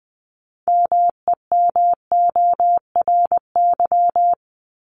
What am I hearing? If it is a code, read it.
Morse code, heard as a keyed tone, about 20 wpm, 700 Hz. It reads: MEMORY
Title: Top 1251-1500 Words 20wpm